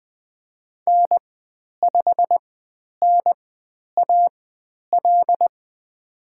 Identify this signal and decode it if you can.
Morse code, heard as a keyed tone, about 20 wpm, spaced as for 10 wpm, 700 Hz. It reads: N5NAL
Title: 109 Lesson 18g - Letter m - Call Signs 20wpm